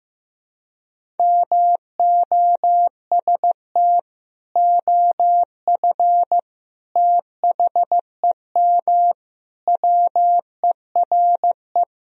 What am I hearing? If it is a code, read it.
Morse code, heard as a keyed tone, about 15 wpm, 700 Hz. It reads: MOST OF THEM WERE
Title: Sentences from Top 1000 Words - Review Part 1 - 15wpm